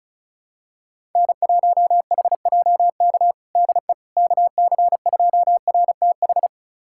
Morse code, heard as a keyed tone, about 34 wpm, 700 Hz. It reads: N1HJK DE KC2RTH
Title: USA Callsign DE Callsign 35wpm